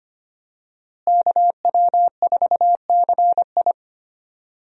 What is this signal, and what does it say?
Morse code, heard as a keyed tone, about 25 wpm, 700 Hz. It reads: KW4CI
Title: USA Call Signs - 5 Characters - Repeated 2x - 25wpm